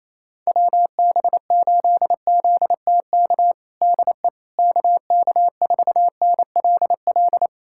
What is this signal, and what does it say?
Morse code, heard as a keyed tone, about 28 wpm, 700 Hz. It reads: WB8ZTK DE KK4NLL
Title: USA Callsign DE Callsign - 6 Characters - 28wpm